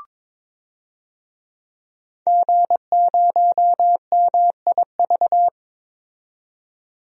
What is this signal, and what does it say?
Morse code, heard as a keyed tone, about 21 wpm, 700 Hz. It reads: G0MIV
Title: DXCC-100 Call Signs - 5 Characters - 22wpm